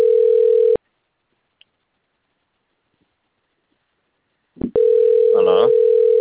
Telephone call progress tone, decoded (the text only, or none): ringback tone